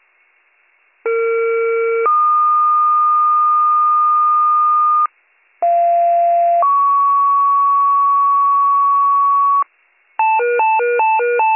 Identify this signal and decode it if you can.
Two-tone page, heard as a one-second tone, then a three-second tone, about 460 then 1220 Hz, about 690 then 1120 Hz. It is heard from 1.0 to 9.6 s.